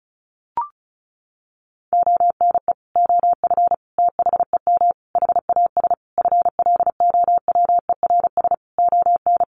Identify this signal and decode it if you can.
Morse code, heard as a keyed tone, about 34 wpm, 700 Hz. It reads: ONE OF THEM HAS FLOWERS ON
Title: Additional Easy Phrases 35wpm